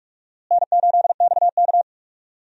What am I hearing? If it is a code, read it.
Morse code, heard as a keyed tone, about 42 wpm, 700 Hz. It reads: N8XK
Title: USA Call Signs - 4 Characters - Repeated 2x - 45wpm